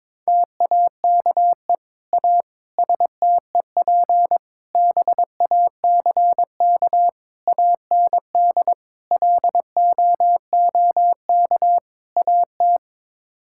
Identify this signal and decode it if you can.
Morse code, heard as a keyed tone, about 22 wpm, 700 Hz. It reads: TAKE A STEP BACK AND LOOK AT
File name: practice / 22wpm / Sets of 7 Words - 22wpm